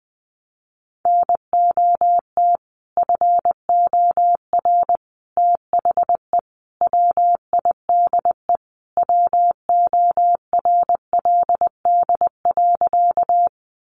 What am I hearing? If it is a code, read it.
Morse code, heard as a keyed tone, about 20 wpm, 700 Hz. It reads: NOT FOR THE WIDE WORLD.
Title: Sentences from Top 900 Words 20wpm